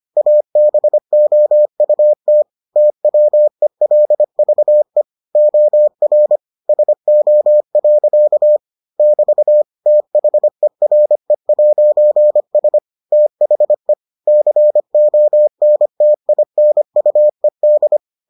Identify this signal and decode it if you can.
Morse code, heard as a keyed tone, about 25 wpm, 600 Hz. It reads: ABOUT TWELVE OR SO. <BT> THERE'S THE CONTINUED